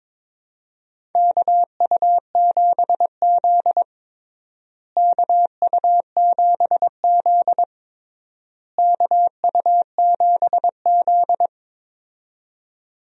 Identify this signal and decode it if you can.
Morse code, heard as a keyed tone, about 21 wpm, 700 Hz. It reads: KU7Z KU7Z KU7Z
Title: USA Call Signs - 4 Characters - Repeated 3x - 22wpm